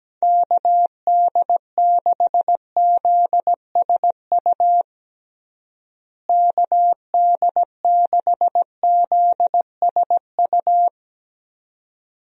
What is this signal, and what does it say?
Morse code, heard as a keyed tone, about 17 wpm, 700 Hz. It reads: KD6ZSU KD6ZSU
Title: USA Call Signs - 6 Characters - Repeated 2x - 17wpm